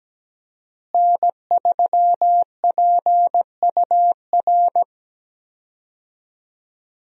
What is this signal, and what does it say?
Morse code, heard as a keyed tone, about 17 wpm, 700 Hz. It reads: N3PUR
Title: USA Call Signs - 5 Characters - Repeated 3x - 17wpm